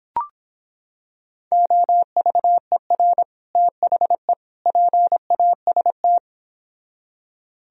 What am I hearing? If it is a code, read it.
Morse code, heard as a keyed tone, about 26 wpm, 700 Hz. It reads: OVER THE PAST